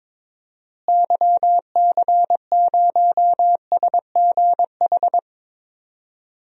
Morse code, heard as a keyed tone, about 21 wpm, 700 Hz. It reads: YC0SGH
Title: DXCC-100 Call Signs - 6 Characters - Repeated 3x - 22wpm